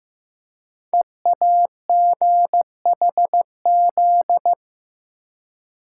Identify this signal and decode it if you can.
Morse code, heard as a keyed tone, about 15 wpm, 700 Hz. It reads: EAGHZ
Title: Code Groups - 5 Letters - 15wpm